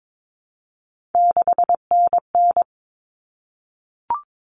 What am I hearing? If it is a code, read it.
Morse code, heard as a keyed tone, about 22 wpm, 700 Hz. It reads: 6NN
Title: Common Cut Numbers - 3-Digit 22wpm